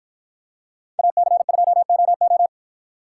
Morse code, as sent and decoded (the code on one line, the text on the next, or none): .- -.-. ..--- --- ---
AC2OO